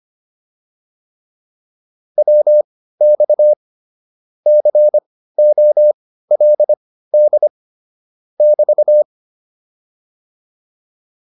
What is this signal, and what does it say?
Morse code, heard as a keyed tone, about 25 wpm, spaced as for 15 wpm, 600 Hz. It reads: WX COLD <BT>